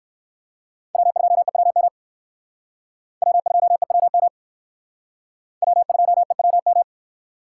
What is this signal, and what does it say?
Morse code, heard as a keyed tone, about 59 wpm, 700 Hz. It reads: W2EWM W2EWM W2EWM